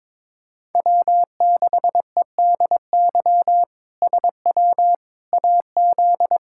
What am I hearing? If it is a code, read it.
Morse code, heard as a keyed tone, about 22 wpm, 700 Hz. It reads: W6EDY SW AZ
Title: K1USN SST QSO Practice - 22wpm